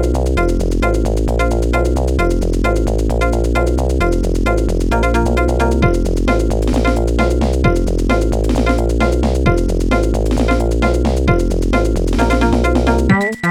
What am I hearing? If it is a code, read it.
There is no code.